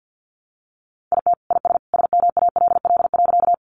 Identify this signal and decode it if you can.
Morse code, heard as a keyed tone, about 50 wpm, 700 Hz. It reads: IT IS SMALL.
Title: Additional Easy Phrases 50wpm